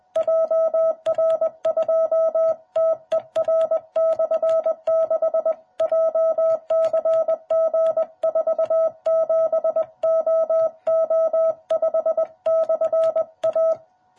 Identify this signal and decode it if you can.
Morse code, heard as a keyed tone, about 20 wpm, 660 Hz. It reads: JR2TER/6 JCG47OO5/A
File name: jh8cbh / 2021cw